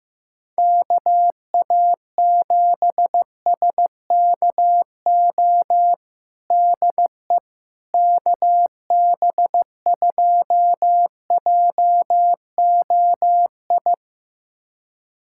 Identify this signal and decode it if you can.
Morse code, heard as a keyed tone, about 16 wpm, 700 Hz. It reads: KA7SKO DE KB2JOI
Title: USA Callsign DE Callsign - 6 Characters - 15wpm